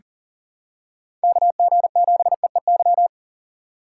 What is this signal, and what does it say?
Morse code, heard as a keyed tone, about 40 wpm, 700 Hz. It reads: KG7EEY
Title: USA Call Signs - 6 Characters - 3x Character Spacing - 40wpm